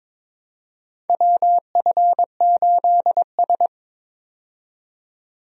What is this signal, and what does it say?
Morse code, heard as a keyed tone, about 21 wpm, 700 Hz. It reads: WF8S
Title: USA Call Signs - 4 Characters - 22wpm